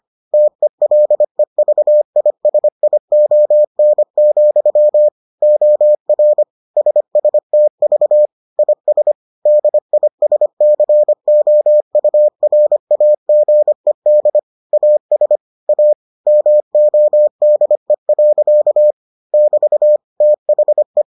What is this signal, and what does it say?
Morse code, heard as a keyed tone, about 25 wpm, 600 Hz. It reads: TELEVISION, OR SSTV IS DISCOURAGED AS A MODE. <BT> THE